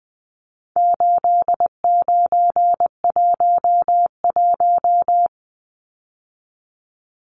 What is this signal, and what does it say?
Morse code, heard as a keyed tone, about 20 wpm, 700 Hz. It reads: 8911
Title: Numbers - 4-Digit 20wpm